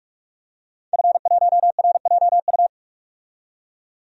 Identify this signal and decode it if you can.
Morse code, heard as a keyed tone, about 46 wpm, 700 Hz. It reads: F1RJU
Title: DXCC-100 Call Signs - 5 Characters - 45wpm